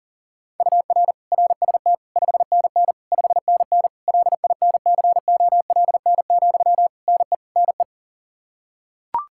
Morse code, heard as a keyed tone, about 40 wpm, 700 Hz. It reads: UR RST 5NN 5NN LINCOLN, NE NE
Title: Practice QSO - Exchange 1 - Beginning - 40wpm